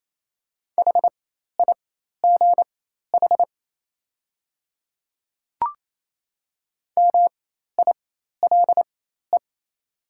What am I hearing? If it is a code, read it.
Morse code, heard as a keyed tone, about 27 wpm, spaced as for 13 wpm, 700 Hz. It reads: HIGH MILE
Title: 4x Character Spacing - Top 500 Words - 28wpm